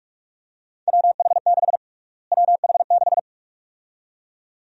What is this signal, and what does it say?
Morse code, heard as a keyed tone, about 44 wpm, 700 Hz. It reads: WH6 WH6
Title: USA Call Sign Prefixes - Repeated 2x - 45wpm